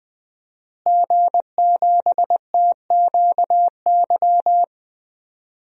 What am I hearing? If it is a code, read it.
Morse code, heard as a keyed tone, about 20 wpm, 700 Hz. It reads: G7TQY
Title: DXCC-100 Call Signs - 5 Characters - Repeated 3x - 20wpm